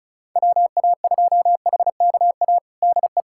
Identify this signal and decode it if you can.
Morse code, heard as a keyed tone, about 35 wpm, 700 Hz. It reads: WA2HKA DE
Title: USA Callsign DE Callsign - 6 Characters - 35wpm